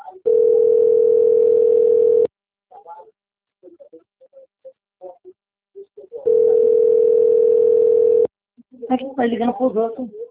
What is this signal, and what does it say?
Ringback tone (North America), 440 + 480 Hz, with speech.
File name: out- - - - - .70568